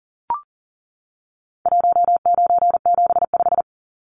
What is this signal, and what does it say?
Morse code, heard as a keyed tone, about 40 wpm, 700 Hz. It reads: 1975
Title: Numbers - 4-Digit 40wpm